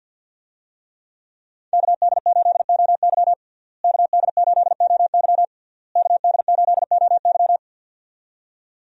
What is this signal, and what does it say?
Morse code, heard as a keyed tone, about 51 wpm, 700 Hz. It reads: KD8OY KD8OY KD8OY
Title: USA Call Signs - 5 Characters - Repeated 3x - 50wpm